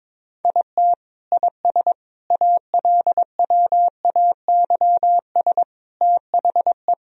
Morse code, heard as a keyed tone, about 22 wpm, 700 Hz. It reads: IT IS ALWAYS THE
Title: Sentences from Top 1000 Words - Review Part 1 - 22wpm